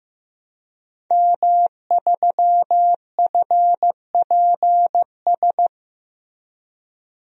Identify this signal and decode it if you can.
Morse code, heard as a keyed tone, about 16 wpm, 700 Hz. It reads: M3FPS
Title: DXCC-100 Call Signs - 5 Characters - 15wpm